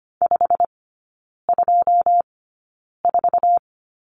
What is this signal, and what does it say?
Morse code, heard as a keyed tone, about 25 wpm, spaced as for 9 wpm, 690 Hz. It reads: 524
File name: Lesson 15e - Number 5 - Three Characters 25wpm